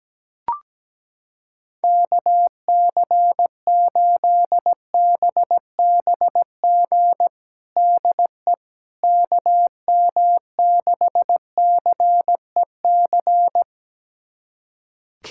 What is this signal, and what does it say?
Morse code, heard as a keyed tone, about 17 wpm, 700 Hz. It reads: KC8BBG DE KM6CEC
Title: USA Callsign DE Callsign 17wpm